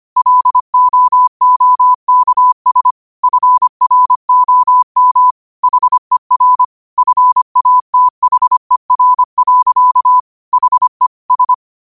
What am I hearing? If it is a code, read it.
Morse code, heard as a keyed tone, about 25 wpm, 1000 Hz. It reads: LOOKS FROM HER FATHER. HE S
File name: qotdfiles / 25wpm